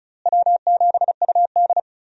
Morse code, heard as a keyed tone, about 36 wpm, 700 Hz. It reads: W7UD
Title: USA Call Signs - 4 Characters - Repeated 2x - 35wpm